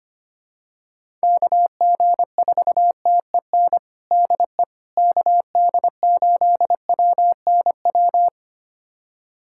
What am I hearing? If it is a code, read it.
Morse code, heard as a keyed tone, about 25 wpm, 700 Hz. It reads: KG4TEN DE KD8WNW
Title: USA Callsign DE Callsign - 6 Characters - 25wpm